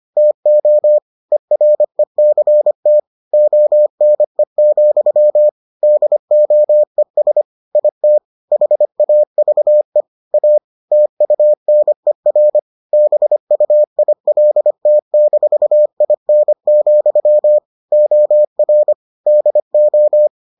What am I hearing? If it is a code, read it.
Morse code, heard as a keyed tone, about 25 wpm, 600 Hz. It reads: TO ERECT ONE, DOES IT HAVE A TUNER BUILT-IN, OR DO